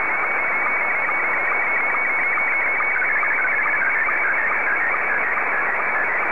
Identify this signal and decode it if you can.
SSTV (slow-scan television), Scottie 1.
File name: sstv / S1-JA5PJ